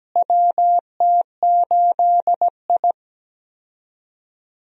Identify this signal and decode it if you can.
Morse code, heard as a keyed tone, about 17 wpm, 700 Hz. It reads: WT8I
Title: USA Call Signs - 4 Characters - Repeated 2x - 17wpm